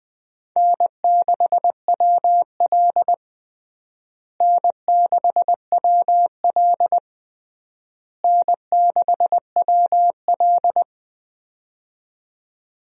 Morse code, heard as a keyed tone, about 20 wpm, 690 Hz. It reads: N6WL N6WL N6WL
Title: USA Call Signs - 4 Characters - Repeated 3x - 20wpm